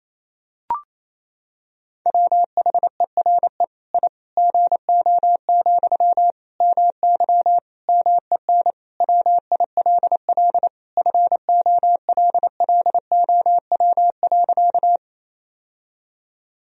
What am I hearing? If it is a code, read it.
Morse code, heard as a keyed tone, about 28 wpm, 700 Hz. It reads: WHERE I GO, MY MEN WILL FOLLOW.
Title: Sentences from Top 1000 Words - Review Part 1 - 28wpm